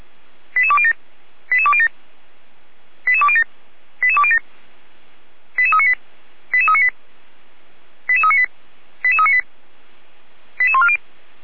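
Five-tone selective call (ZVEI), heard as quick runs of five tones, tones of about 70 ms. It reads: 88207 88207 88207 88207 88398 88398 88398 88398 88140
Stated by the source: ZVEI five-tone selective call received on an SDR, from the fire brigade (Feuerwehr) of Orlamünde.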